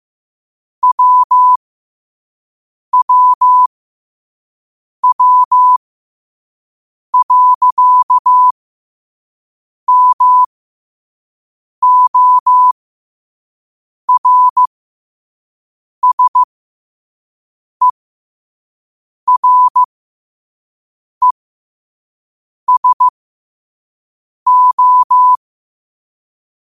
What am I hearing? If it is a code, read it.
Morse code, heard as a keyed tone, about 15 wpm, spaced as for 5 wpm, 1000 Hz. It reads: WWW.MORSERESO
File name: qotdfiles / 5wpm